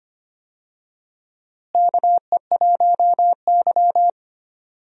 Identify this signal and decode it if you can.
Morse code, heard as a keyed tone, about 25 wpm, 700 Hz. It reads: KE1Y
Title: USA Call Signs - 4 Characters - Repeated 2x - 25wpm